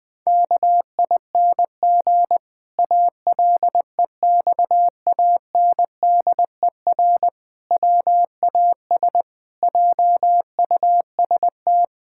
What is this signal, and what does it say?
Morse code, heard as a keyed tone, about 19 wpm, 700 Hz. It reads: KING ALEXANDER WAS JUST
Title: Pangram Challenge v2 - 20wpm